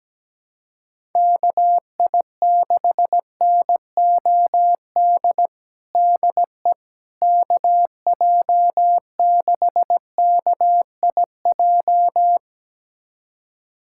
Morse code, heard as a keyed tone, about 17 wpm, 700 Hz. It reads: KI6NOD DE KJ6KIJ
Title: USA Callsign DE Callsign - 6 Characters - 17wpm